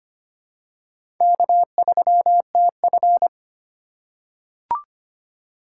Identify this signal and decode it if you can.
Morse code, heard as a keyed tone, about 25 wpm, 700 Hz. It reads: K3TF
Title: K1USN SST QSO Practice - 25wpm